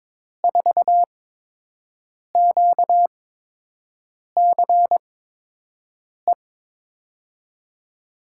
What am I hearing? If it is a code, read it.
Morse code, heard as a keyed tone, about 22 wpm, spaced as for 6 wpm, 700 Hz. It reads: 4QCE
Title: USA Call Signs - 6 Characters - 8x Character Spacing - 22wpm